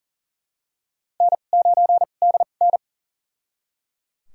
Morse code, heard as a keyed tone, about 37 wpm, spaced as for 28 wpm, 700 Hz. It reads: N9DN
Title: USA Call Signs - 4 Characters - 2x Character Spacing - 40wpm